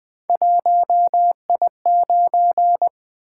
Morse code, heard as a keyed tone, about 20 wpm, 700 Hz. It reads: 1I9
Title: Three Letter-Number Combinations - 20wpm